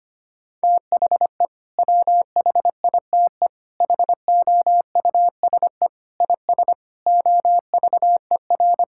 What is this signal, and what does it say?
Morse code, heard as a keyed tone, about 25 wpm, 700 Hz. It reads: THE WHITE HOUSE IS OVER
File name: practice / 25wpm / Sentences with 3-Word Phrases - 25wpm